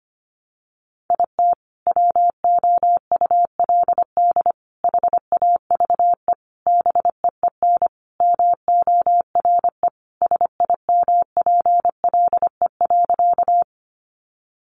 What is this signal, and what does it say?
Morse code, heard as a keyed tone, about 25 wpm, 700 Hz. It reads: IT WOULD HAVE BEEN MORE SIMPLE.
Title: Sentences from Top 400 Words 25wpm